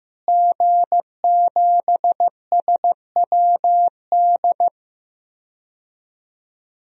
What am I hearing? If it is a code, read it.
Morse code, heard as a keyed tone, about 15 wpm, 700 Hz. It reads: G7SWD
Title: DXCC-100 Call Signs - 5 Characters - Repeated 2x - 15wpm